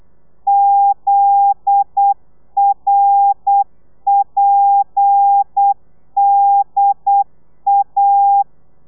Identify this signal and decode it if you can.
Morse code, heard as a keyed tone, about 8 wpm, 800 Hz. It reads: ZRPDA
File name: telegrafia / i2tfj / Lez10 35